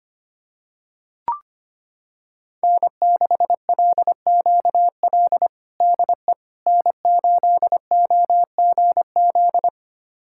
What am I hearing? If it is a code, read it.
Morse code, heard as a keyed tone, about 25 wpm, 700 Hz. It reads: N6LQL DE N8OGZ
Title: USA Callsign DE Callsign - 5 Characters - 25wpm